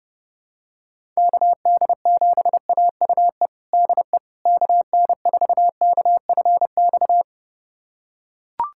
Morse code, heard as a keyed tone, about 30 wpm, 700 Hz. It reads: KD7AUE DE KN4KFX
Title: USA Callsign DE Callsign - 6 Characters - 30wpm